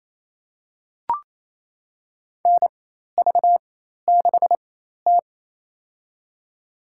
Morse code, heard as a keyed tone, about 28 wpm, spaced as for 13 wpm, 700 Hz. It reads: NV6T
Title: USA Call Signs - 4 Characters - 4x Character Spacing - 28wpm